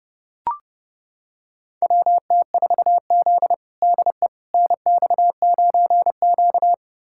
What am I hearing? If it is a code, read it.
Morse code, heard as a keyed tone, about 32 wpm, 700 Hz. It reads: WT4Z DE NX9Q
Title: USA Callsign DE Callsign - 4 Characters - 30wpm